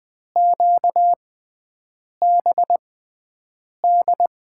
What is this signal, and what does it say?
Morse code, heard as a keyed tone, about 20 wpm, spaced as for 7 wpm, 700 Hz. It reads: QBD
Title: USA Call Signs - 6 Characters - 6x Character Spacing - 20wpm